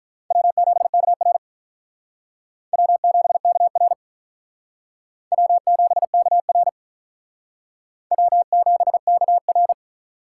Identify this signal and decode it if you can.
Morse code, heard as a keyed tone, about 42 wpm, 700 Hz. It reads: W7KR W7KR W7KR W7KR
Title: Speed Racing - USA 1x2 Call Signs 35wpm